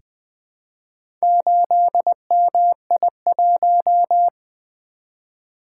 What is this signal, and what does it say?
Morse code, heard as a keyed tone, about 20 wpm, 700 Hz. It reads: 8MI1
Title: Four Letter-Number Combinations - 20wpm